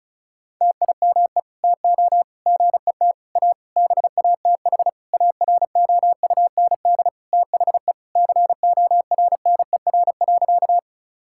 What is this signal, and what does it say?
Morse code, heard as a keyed tone, about 34 wpm, 700 Hz. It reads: TIME TO GET A BATH AROUND THE CORNER.
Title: Sentences from Top 1600 Words 35wpm